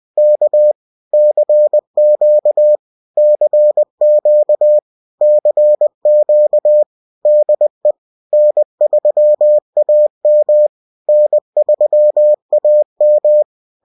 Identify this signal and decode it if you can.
Morse code, heard as a keyed tone, about 20 wpm, 600 Hz. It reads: K CQ CQ CQ DE N3AM N3AM